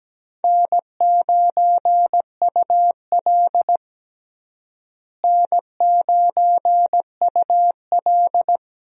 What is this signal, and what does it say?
Morse code, heard as a keyed tone, about 17 wpm, 700 Hz. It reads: N9UL N9UL
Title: USA Call Signs - 4 Characters - Repeated 3x - 17wpm